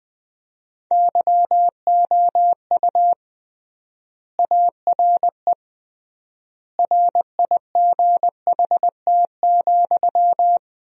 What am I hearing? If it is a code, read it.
Morse code, heard as a keyed tone, about 21 wpm, 700 Hz. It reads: YOU ARE RIGHT,
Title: 3x Word-Spacing - Sentences from Top 400 Words - 20wpm